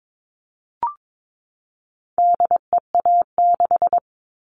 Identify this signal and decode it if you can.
Morse code, heard as a keyed tone, about 22 wpm, 700 Hz. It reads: DEA6
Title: All Cut Numbers - 4-Digit 22wpm